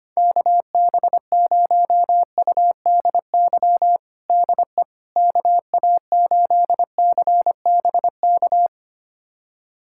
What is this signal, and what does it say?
Morse code, heard as a keyed tone, about 25 wpm, 700 Hz. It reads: KB0UDY DE KA8CBK
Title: USA Callsign DE Callsign - 6 Characters - 25wpm